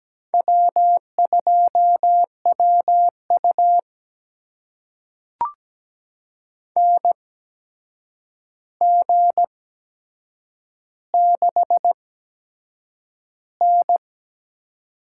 Morse code, heard as a keyed tone, about 17 wpm, spaced as for 7 wpm, 700 Hz. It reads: W2WU NG6N
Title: USA Call Signs - 8x Character Spacing - 17wpm